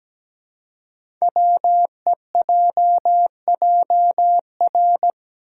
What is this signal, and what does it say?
Morse code, heard as a keyed tone, about 17 wpm, 700 Hz. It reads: WEJJR
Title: Code Groups - 5 Letters - 17wpm